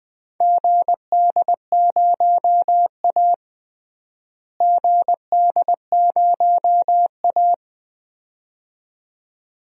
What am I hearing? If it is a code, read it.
Morse code, heard as a keyed tone, about 20 wpm, 700 Hz. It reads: GD0A GD0A
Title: DXCC-100 Call Signs - 4 Characters - Repeated 2x - 20wpm